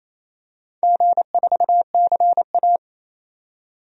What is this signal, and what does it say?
Morse code, heard as a keyed tone, about 28 wpm, 690 Hz. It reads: G4CA
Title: DXCC-100 Call Signs - 4 Characters - 28wpm